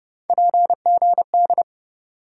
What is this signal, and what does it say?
Morse code, heard as a keyed tone, about 30 wpm, 700 Hz. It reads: PGD
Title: USA Call Sign Postfixes - 3 Letters 30wpm